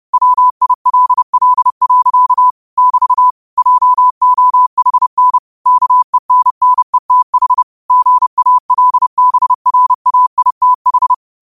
Morse code, heard as a keyed tone, about 30 wpm, 1000 Hz. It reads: WILL. <BT> JOHN KENNETH GALBRAITH